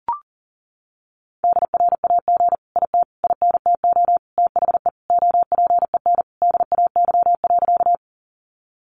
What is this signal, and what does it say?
Morse code, heard as a keyed tone, about 40 wpm, 700 Hz. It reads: DRAG IT INTO THE OPEN DAY.
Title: Sentences from Top 1500 Words 40wpm